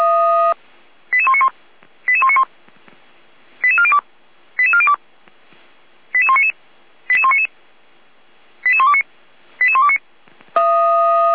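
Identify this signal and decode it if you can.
Five-tone selective call (ZVEI), heard as quick runs of five tones, tones of about 70 ms. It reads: 88191 88191 88402 88402 88199 88199 88129 88129
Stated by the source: Received by SDR, ZVEI five-tone selective call.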